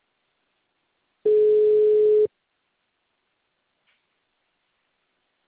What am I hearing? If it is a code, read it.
Ringback tone (Europe, 425 Hz).
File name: out- -2012- - - .2